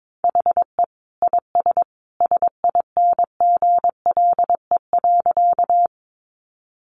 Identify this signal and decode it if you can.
Morse code, heard as a keyed tone, about 22 wpm, 700 Hz. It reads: HE IS SINGLE.
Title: Sentences from Top 700 Words 22wpm